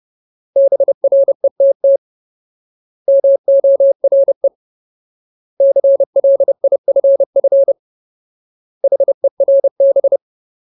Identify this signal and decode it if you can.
Morse code, heard as a keyed tone, about 30 wpm, 550 Hz. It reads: BRETT MORE CLIFF HERB
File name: Ultimate Word List / 30WPM Ultimate